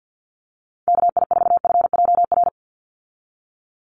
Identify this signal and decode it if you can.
Morse code, heard as a keyed tone, about 50 wpm, 700 Hz. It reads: KI4FJR
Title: USA Call Signs 50wpm